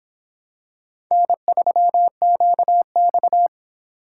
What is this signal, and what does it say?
Morse code, heard as a keyed tone, about 23 wpm, 700 Hz. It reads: N3QX